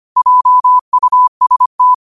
Morse code, heard as a keyed tone, about 25 wpm, 1000 Hz. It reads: JUST